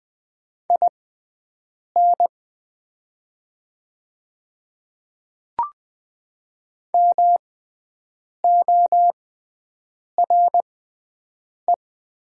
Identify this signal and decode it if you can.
Morse code, heard as a keyed tone, about 21 wpm, spaced as for 7 wpm, 700 Hz. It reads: IN MORE
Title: 6x Character Spacing - Top 200 Words - 20wpm